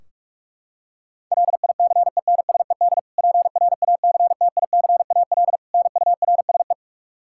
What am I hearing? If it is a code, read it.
Morse code, heard as a keyed tone, about 41 wpm, 700 Hz. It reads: LICENSED PRACTICAL NURSE
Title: Practice QSO - Exchange 3 - More Info - 45wpm